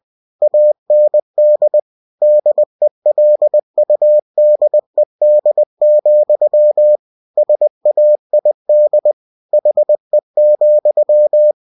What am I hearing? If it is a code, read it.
Morse code, heard as a keyed tone, about 20 wpm, 600 Hz. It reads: AND DELUDED, SAID HE,
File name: cw-books / winnie-the-pooh / winnie-20wpm-page-0025